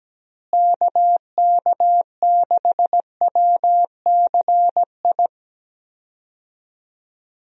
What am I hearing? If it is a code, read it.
Morse code, heard as a keyed tone, about 17 wpm, 700 Hz. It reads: KK6WCI
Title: USA Call Signs - 6 Characters - Repeated 2x - 17wpm